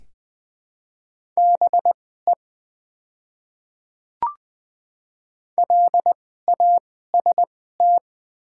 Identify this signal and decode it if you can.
Morse code, heard as a keyed tone, about 20 wpm, spaced as for 14 wpm, 700 Hz. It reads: BE LAST
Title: 2x Character Spacing - Top 300 Words - 20wpm